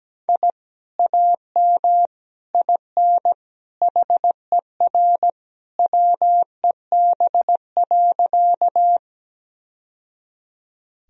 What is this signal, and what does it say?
Morse code, heard as a keyed tone, about 17 wpm, 700 Hz. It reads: I AM IN HER WEB.
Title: Sentences from Top 1000 Words - Review Part 4 - 17wpm